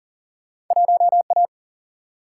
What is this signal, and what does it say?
Morse code, heard as a keyed tone, about 40 wpm, 700 Hz. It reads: 1A
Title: Single-Letter Single-Number 40wpm